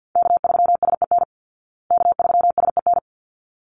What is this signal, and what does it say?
Morse code, heard as a keyed tone, about 48 wpm, 700 Hz. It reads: K3SEN K3SEN
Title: Call Signs Repeated 3x - 50wpm